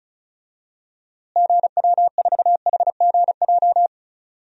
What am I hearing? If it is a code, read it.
Morse code, heard as a keyed tone, about 34 wpm, 700 Hz. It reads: GW4HGJ
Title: DXCC-100 Call Signs - 6 Characters - Repeated 2x - 35wpm